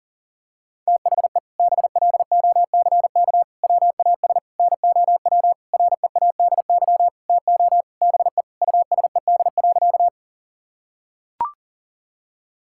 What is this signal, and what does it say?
Morse code, heard as a keyed tone, about 40 wpm, 700 Hz. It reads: THE BLOCK WAS NOW READY TO BE USED.
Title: Sentences from Top 1600 Words 40wpm